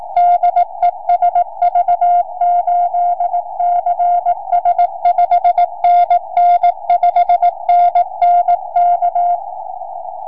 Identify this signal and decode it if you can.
Morse code, heard as a keyed tone, about 18 wpm, 700 Hz. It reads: DE SV8CS 5NN 5NN K
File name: sv8cs 4m